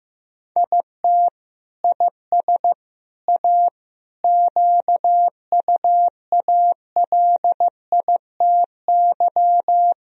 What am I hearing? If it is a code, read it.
Morse code, heard as a keyed tone, about 15 wpm, 700 Hz. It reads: IT IS A QUALITY
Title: Sentences from Top 1000 Words - Review Part 4 - 15wpm